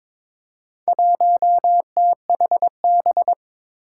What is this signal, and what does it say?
Morse code, heard as a keyed tone, about 22 wpm, 700 Hz. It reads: 1THB
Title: Four Challenging Letter-Number Combinations 22wpm